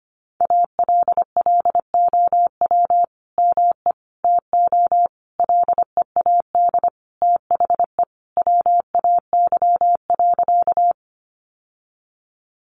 Morse code, heard as a keyed tone, about 25 wpm, 700 Hz. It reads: ALLOW ME TO LEAD THE WAY.
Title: Sentences from Top 700 Words 25wpm